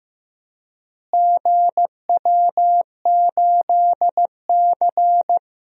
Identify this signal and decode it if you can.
Morse code, heard as a keyed tone, about 15 wpm, 700 Hz. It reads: GW8C
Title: DXCC-100 Call Signs - 4 Characters - Repeated 2x - 15wpm